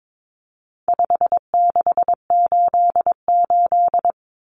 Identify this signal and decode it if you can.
Morse code, heard as a keyed tone, about 22 wpm, 700 Hz. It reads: 5688
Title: Common Cut Numbers - 4-Digit 22wpm